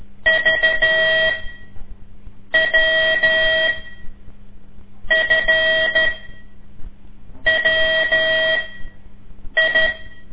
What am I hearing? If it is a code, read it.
Morse code, heard as a keyed tone, about 13 wpm, spaced as for 5 wpm, 640 Hz, with noise. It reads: VWFWI
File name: ik1weg2 / CORSI CW / lez5-2